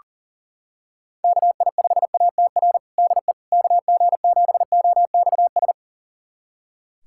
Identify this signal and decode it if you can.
Morse code, heard as a keyed tone, about 40 wpm, 700 Hz. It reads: KI5ATR DE KG7OXS
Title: USA Callsign DE Callsign 40wpm